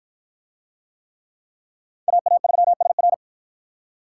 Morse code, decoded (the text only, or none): AA3SR